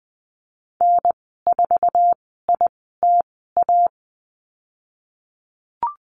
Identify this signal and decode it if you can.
Morse code, heard as a keyed tone, about 20 wpm, spaced as for 15 wpm, 700 Hz. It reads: N4ITA
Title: 055 Lesson 10g - Letter r - Call Signs 20wpm